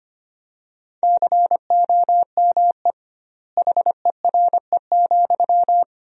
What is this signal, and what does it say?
Morse code, heard as a keyed tone, about 25 wpm, 700 Hz. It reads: COME HERE,
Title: 2x Word-Spacing - Sentences from Top 300 Words - 25wpm